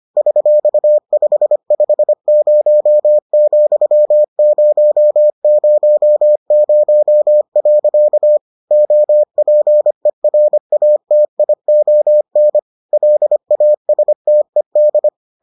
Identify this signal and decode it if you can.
Morse code, heard as a keyed tone, about 25 wpm, 600 Hz. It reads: $550,000. OPERATION LASTED